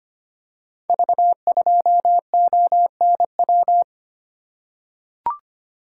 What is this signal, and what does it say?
Morse code, heard as a keyed tone, about 25 wpm, 700 Hz. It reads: V2ONW
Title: Code Groups - 5 Letter:Numbers - 25wpm